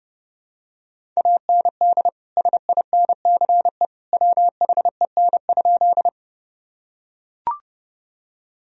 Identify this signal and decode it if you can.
Morse code, heard as a keyed tone, about 32 wpm, 700 Hz. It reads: AND SINCE WHEN?
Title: Sentences from Top 1000 Words - Review Part 1 - 30wpm